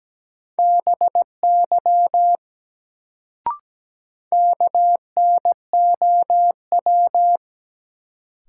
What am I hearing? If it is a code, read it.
Morse code, heard as a keyed tone, about 17 wpm, 700 Hz. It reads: BY KNOW
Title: Sentences from Top 100 Words - New Word Drill - 17wpm